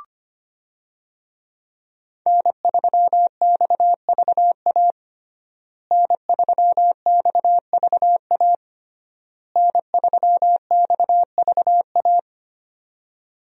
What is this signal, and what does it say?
Morse code, heard as a keyed tone, about 25 wpm, 700 Hz. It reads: N3XVA N3XVA N3XVA
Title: USA Call Signs - 5 Characters - Repeated 3x - 25wpm